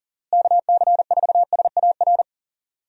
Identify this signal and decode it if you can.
Morse code, heard as a keyed tone, about 40 wpm, 700 Hz. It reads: KC4SAR
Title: USA Call Signs - 6 Characters - Repeated 2x - 40wpm